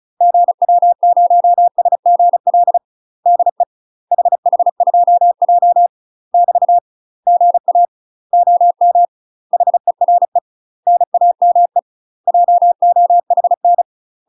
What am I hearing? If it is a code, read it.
Morse code, heard as a keyed tone, about 36 wpm, 700 Hz. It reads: GW0SGL DE HH2J <BT> GA OM HERE NAME JOHN
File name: UK MASTER QSOs TRAINING / 35WPM / UK MASTER QSOs 0020